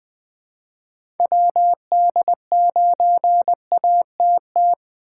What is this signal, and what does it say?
Morse code, heard as a keyed tone, about 20 wpm, 700 Hz. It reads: WD9ATT
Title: Speed Racing - USA 2x3 Call Signs 17wpm